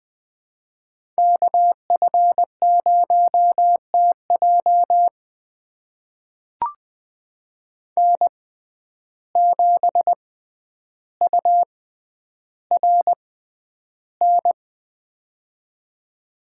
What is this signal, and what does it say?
Morse code, heard as a keyed tone, about 20 wpm, spaced as for 10 wpm, 700 Hz. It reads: KF0TJ N7URN
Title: USA Call Signs - 5 Characters - 6x Character Spacing - 20wpm